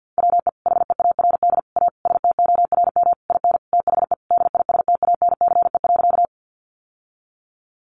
Morse code, heard as a keyed tone, about 50 wpm, 700 Hz. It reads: WE HEARD A STORM IN THE DISTANCE.
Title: Sentences from Top 1500 Words 50wpm